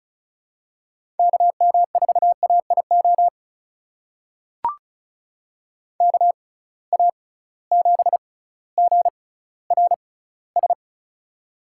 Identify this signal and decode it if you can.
Morse code, heard as a keyed tone, about 34 wpm, spaced as for 18 wpm, 700 Hz. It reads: KM4AIO KA7GRS
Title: USA Call Signs - 6 Characters - 6x Character Spacing - 35wpm